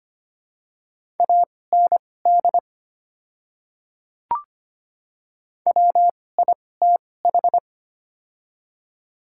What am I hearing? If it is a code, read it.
Morse code, heard as a keyed tone, about 25 wpm, spaced as for 18 wpm, 690 Hz. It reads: AND WITH
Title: 2x Character Spacing - Top 100 Words - 25wpm